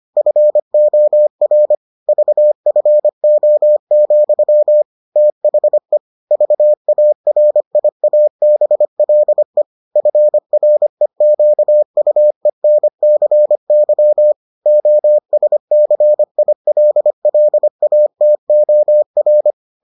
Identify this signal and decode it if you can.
Morse code, heard as a keyed tone, about 25 wpm, 600 Hz. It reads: FOR VFO, THE VARIABLE FREQUENCY OSCILLATOR